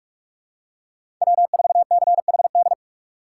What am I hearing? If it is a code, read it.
Morse code, heard as a keyed tone, about 43 wpm, 700 Hz. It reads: W4CHD
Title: USA Call Signs - 5 Characters - 2x Character Spacing - 45wpm